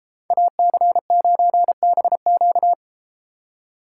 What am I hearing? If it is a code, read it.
Morse code, heard as a keyed tone, about 32 wpm, 700 Hz. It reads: AC9BQ